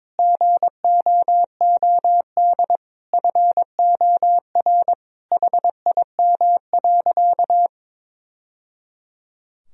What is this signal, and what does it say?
Morse code, heard as a keyed tone, about 22 wpm, 700 Hz. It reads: GOOD FOR HIM.
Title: Sentences from Top 1000 Words - Review Part 1 - 22wpm